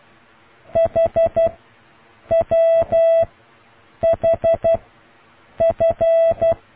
Morse code, heard as a keyed tone, about 12 wpm, spaced as for 7 wpm, 660 Hz. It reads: HWHF